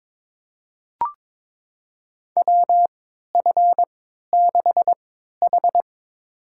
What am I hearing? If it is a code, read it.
Morse code, heard as a keyed tone, about 21 wpm, spaced as for 12 wpm, 700 Hz. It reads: WF6H
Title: USA Call Signs - 4 Characters - 3x Character Spacing - 22wpm